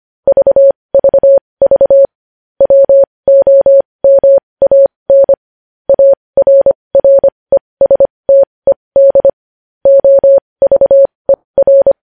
Morse code, heard as a keyed tone, about 25 wpm, spaced as for 20 wpm, 550 Hz. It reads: VVV WOMAN ARRESTED OVER
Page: News Headlines in Morse Code at 20 WPM